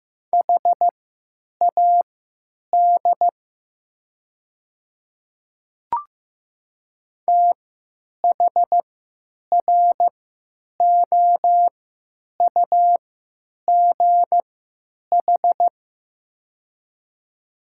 Morse code, heard as a keyed tone, about 15 wpm, spaced as for 9 wpm, 700 Hz. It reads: HAD THROUGH
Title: 3x Character Spacing - Top 200 Words - 15wpm